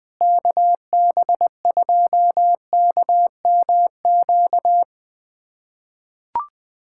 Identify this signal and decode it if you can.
Morse code, heard as a keyed tone, about 20 wpm, 700 Hz. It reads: KB2KMQ
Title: USA Callsign DE Callsign - 6 Characters - 20wpm